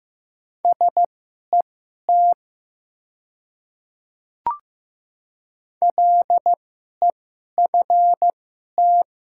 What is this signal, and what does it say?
Morse code, heard as a keyed tone, about 15 wpm, spaced as for 11 wpm, 700 Hz. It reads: SET LEFT
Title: 2x Character Spacing - Top 300 Words - 15wpm